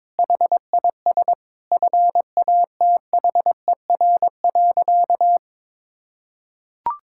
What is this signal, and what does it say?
Morse code, heard as a keyed tone, about 22 wpm, 700 Hz. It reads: HIS FATHER.
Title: Sentences from Top 1000 Words - Review Part 6 - 22wpm